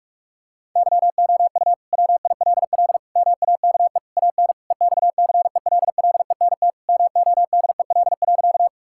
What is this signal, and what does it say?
Morse code, heard as a keyed tone, about 42 wpm, 700 Hz. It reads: YOU WILL MAKE AN EXCELLENT MODEL.
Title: Sentences from Top 1000 Words - Review Part 5 - 45wpm